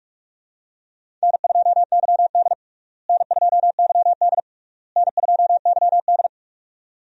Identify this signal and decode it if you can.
Morse code, heard as a keyed tone, about 46 wpm, 700 Hz. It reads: N2YD N2YD N2YD
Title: USA Call Signs - 4 Characters - Repeated 3x - 45wpm